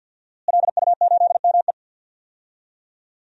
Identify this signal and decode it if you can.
Morse code, heard as a keyed tone, about 52 wpm, 700 Hz. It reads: LU8ME
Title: DXCC-100 Call Signs - 5 Characters - Repeated 2x - 50wpm